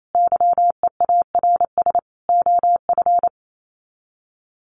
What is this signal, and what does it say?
Morse code, heard as a keyed tone, about 28 wpm, 700 Hz. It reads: YEARS OF
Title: Sets of 5 Words Encore 28wpm